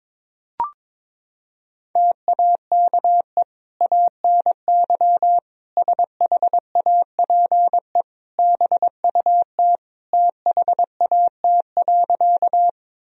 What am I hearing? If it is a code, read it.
Morse code, heard as a keyed tone, about 22 wpm, 700 Hz. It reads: TAKE ANY SHAPE BUT THAT.
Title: Sentences from Top 1000 Words - Review Part 3 - 22wpm